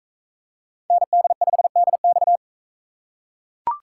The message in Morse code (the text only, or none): ND5DX